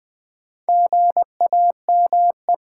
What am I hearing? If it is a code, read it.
Morse code, heard as a keyed tone, about 20 wpm, 700 Hz. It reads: GAME
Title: Easy Words with Hints - 20wpm